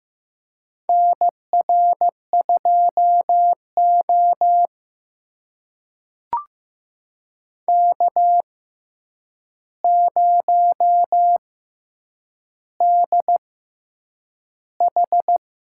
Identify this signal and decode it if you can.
Morse code, heard as a keyed tone, about 15 wpm, spaced as for 8 wpm, 700 Hz. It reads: NR2O K0DH
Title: USA Call Signs - 4 Characters - 6x Character Spacing - 15wpm